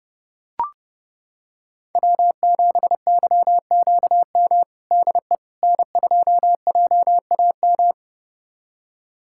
Morse code, heard as a keyed tone, about 30 wpm, 700 Hz. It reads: W7YQM DE N2JAM
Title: USA Callsign DE Callsign - 5 Characters - 30wpm